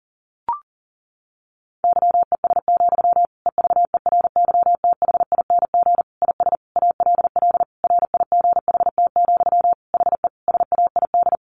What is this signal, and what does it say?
Morse code, heard as a keyed tone, about 40 wpm, 700 Hz. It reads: YES, EVERYTHING IS ALL RIGHT, HE SAID
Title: Sentences from Top 1100 Words 40wpm